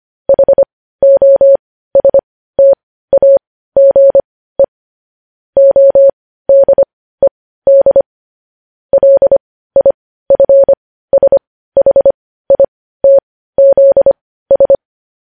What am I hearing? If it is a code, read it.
Morse code, heard as a keyed tone, about 25 wpm, spaced as for 15 wpm, 550 Hz. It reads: HOSTAGE ODED LIFSHITZS
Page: news headlines in morse code at 15 wpm